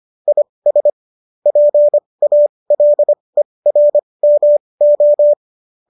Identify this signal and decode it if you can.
Morse code, heard as a keyed tone, about 25 wpm, spaced as for 20 wpm, 590 Hz. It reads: IS PALERMO